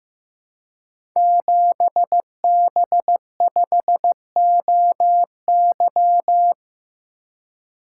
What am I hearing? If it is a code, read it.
Morse code, heard as a keyed tone, about 15 wpm, 700 Hz. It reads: 7B5OY
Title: Code Groups - 5 Letter:Numbers - 15wpm